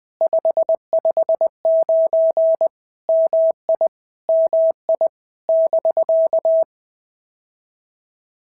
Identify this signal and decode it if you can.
Morse code, heard as a keyed tone, about 20 wpm, 650 Hz. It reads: 559 MI MI <BK>
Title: POTA QSO Practice - 20wpm